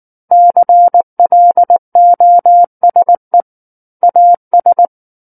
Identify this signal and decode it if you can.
Morse code, heard as a keyed tone, about 19 wpm, 700 Hz. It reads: CLOSE AS
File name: Book One / 10-Chapter 10 - 19 WPM